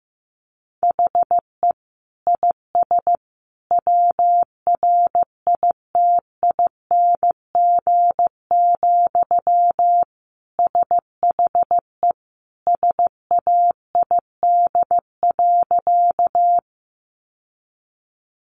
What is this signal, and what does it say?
Morse code, heard as a keyed tone, about 15 wpm, 700 Hz. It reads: HE IS WRITING, SHE SAID.
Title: Sentences from Top 800 Words 15wpm